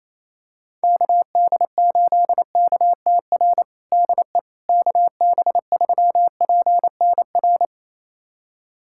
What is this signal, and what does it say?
Morse code, heard as a keyed tone, about 28 wpm, 700 Hz. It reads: KD8KTR DE KB3PNR
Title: USA Callsign DE Callsign - 6 Characters - 28wpm